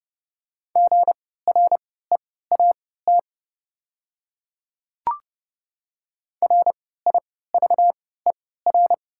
Morse code, heard as a keyed tone, about 30 wpm, spaced as for 17 wpm, 700 Hz. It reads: GREAT RIVER
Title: 3x Character Spacing - Top 300 Words - 30wpm